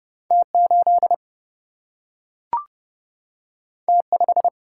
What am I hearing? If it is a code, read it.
Morse code, heard as a keyed tone, about 30 wpm, 700 Hz. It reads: T 8 T 5
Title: Single-Letter Single-Number 30wpm